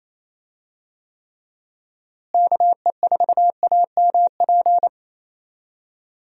Morse code, heard as a keyed tone, about 27 wpm, 700 Hz. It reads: KE4AMP
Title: USA Call Signs - 6 Characters - Repeated 2x - 28wpm